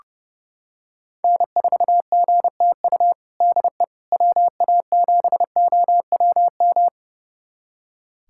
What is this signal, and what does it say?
Morse code, heard as a keyed tone, about 30 wpm, 700 Hz. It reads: N4GTU DE WA7OWM
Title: USA Callsign DE Callsign 30wpm